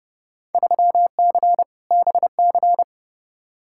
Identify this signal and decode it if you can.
Morse code, heard as a keyed tone, about 30 wpm, 700 Hz. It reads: 3C BC
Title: Field Day Exchanges Encore - 30wpm